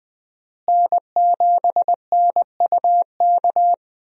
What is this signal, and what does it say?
Morse code, heard as a keyed tone, about 21 wpm, 700 Hz. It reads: N7NUK
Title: USA Call Signs - 5 Characters - Repeated 3x - 20wpm